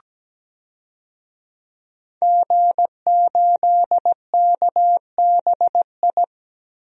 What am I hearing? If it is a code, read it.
Morse code, heard as a keyed tone, about 17 wpm, 700 Hz. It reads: G8KBI
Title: DXCC-100 Call Signs - 5 Characters - 17wpm